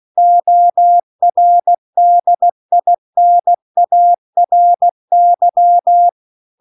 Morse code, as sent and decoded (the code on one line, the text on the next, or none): --- .-. -.. .. -. .- .-. -.--
ORDINARY